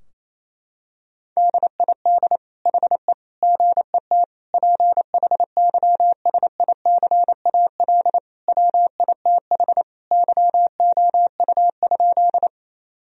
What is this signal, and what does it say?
Morse code, heard as a keyed tone, about 28 wpm, 700 Hz. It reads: DID HE GET PHYSICAL WITH YOU?
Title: Sentences from Top 1000 Words - Review Part 4 - 28wpm